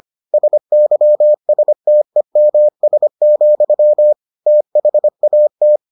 Morse code, heard as a keyed tone, about 25 wpm, 600 Hz. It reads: SYSTEMS, THAT